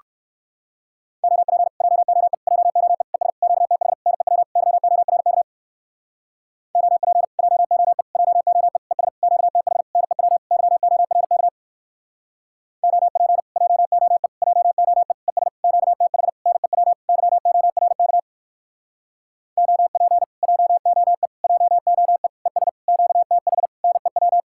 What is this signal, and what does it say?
Morse code, heard as a keyed tone, about 58 wpm, 700 Hz. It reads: OP JOE JOE ES QTH NEW YORK OP JOE JOE ES QTH NEW YORK OP JOE JOE ES QTH NEW YORK OP JOE JOE ES QTH NEW